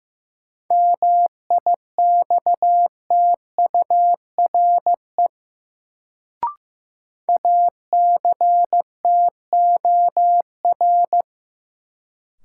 Morse code, heard as a keyed tone, about 16 wpm, 700 Hz. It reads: MIXTURE ACTOR
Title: Sentences from Top 1500 Words - New Word Drill Alt - 15wpm